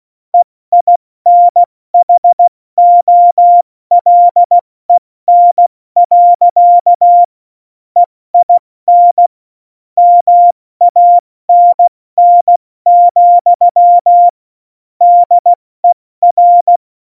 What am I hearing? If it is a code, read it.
Morse code, heard as a keyed tone, about 16 wpm, spaced as for 14 wpm, 700 Hz. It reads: EINHOLEN. EIN MANN, DER
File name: Morsekurs / Witze / Witz 11 Juli 17, Tempo 70 F80